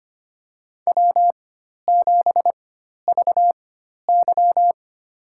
Morse code, heard as a keyed tone, about 25 wpm, spaced as for 12 wpm, 700 Hz. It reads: W7VY
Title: USA Call Signs - 4x Character Spacing - 25wpm